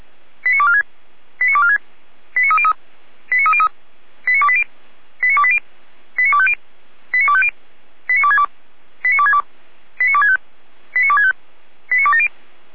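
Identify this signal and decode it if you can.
Five-tone selective call (ZVEI), heard as quick runs of five tones, tones of about 70 ms. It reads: 89246 89246 89303 89303 89280 89280 89250 89250 89262 89262 89265 89265 89270
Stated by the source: ZVEI five-tone selective call received by SDR, from fire brigade Feuerwehr Orlamünde.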